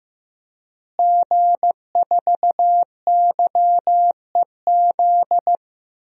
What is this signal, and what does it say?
Morse code, heard as a keyed tone, about 16 wpm, 700 Hz. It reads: G4YEZ
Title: DXCC-100 Call Signs - 5 Characters - Repeated 3x - 15wpm